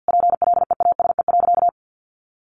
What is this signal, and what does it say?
Morse code, heard as a keyed tone, about 50 wpm, 700 Hz. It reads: PLEASE.
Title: Sentences from Top 1500 Words 50wpm